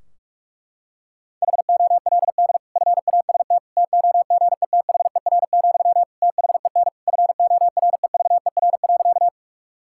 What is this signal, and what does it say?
Morse code, heard as a keyed tone, about 41 wpm, 700 Hz. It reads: HOLD FAST TOGETHER, THEN FOREVER.
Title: Sentences from Top 1600 Words 45wpm